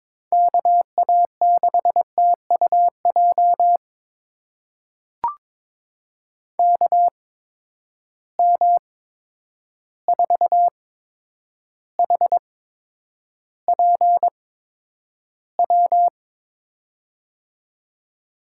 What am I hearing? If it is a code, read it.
Morse code, heard as a keyed tone, about 22 wpm, spaced as for 9 wpm, 700 Hz. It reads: KA6TUJ KM4HPW
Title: USA Call Signs - 6 Characters - 8x Character Spacing - 22wpm